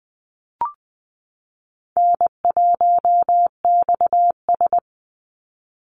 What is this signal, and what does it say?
Morse code, heard as a keyed tone, about 20 wpm, 700 Hz. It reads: N1XS
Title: Easy Call Signs Repeated 3x - 20wpm